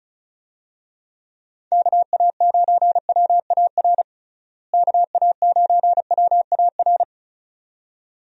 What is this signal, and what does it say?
Morse code, heard as a keyed tone, about 36 wpm, 700 Hz. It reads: KA9WAR KA9WAR
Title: USA Call Signs - 6 Characters - Repeated 2x - 35wpm